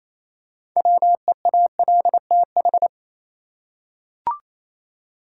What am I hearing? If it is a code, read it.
Morse code, heard as a keyed tone, about 28 wpm, 700 Hz. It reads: WEALTH
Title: Six Letter Words - 28wpm